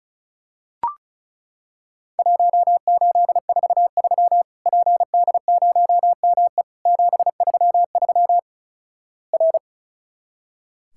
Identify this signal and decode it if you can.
Morse code, heard as a keyed tone, about 34 wpm, 700 Hz. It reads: 1843 PD0ME 733 R
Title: Worked All Europe DX Contest - QTC Exchange - 35wpm